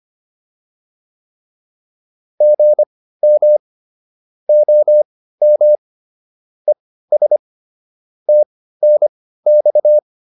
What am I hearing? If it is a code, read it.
Morse code, heard as a keyed tone, about 25 wpm, spaced as for 15 wpm, 600 Hz. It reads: GM OM ES TNX